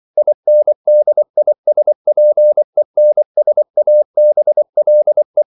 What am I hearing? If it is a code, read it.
Morse code, heard as a keyed tone, about 24 wpm, 600 Hz. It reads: INDISPENSABLE
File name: CW / Practice / EnglishQuotations0324